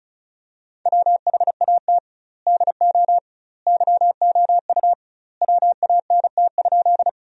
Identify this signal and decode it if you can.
Morse code, heard as a keyed tone, about 34 wpm, 700 Hz. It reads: WHAT DO YOU WANT?
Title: 2x Word-Spacing - Sentences from Top 200 Words - 35wpm